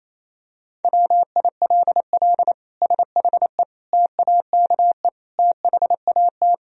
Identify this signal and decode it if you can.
Morse code, heard as a keyed tone, about 28 wpm, 700 Hz. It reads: WILL SHE TAKE THAT
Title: Sentences from Top 1000 Words - Review Part 4 - 28wpm